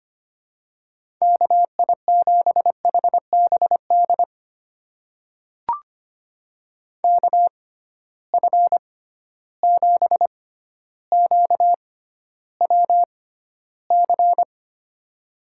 Morse code, heard as a keyed tone, about 25 wpm, spaced as for 13 wpm, 700 Hz. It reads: KI7HBD KF7QWC
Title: USA Call Signs - 6 Characters - 6x Character Spacing - 25wpm